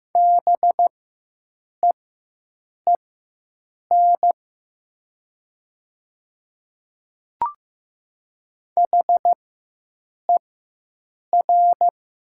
Morse code, heard as a keyed tone, about 15 wpm, spaced as for 7 wpm, 700 Hz. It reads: BEEN HER
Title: 4x Character Spacing - Top 100 Words - 15wpm